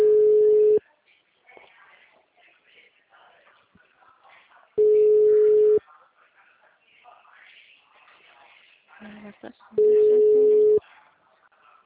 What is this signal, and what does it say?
Ringback tone (Europe, 425 Hz), with speech.
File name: out- -2002- - - .73418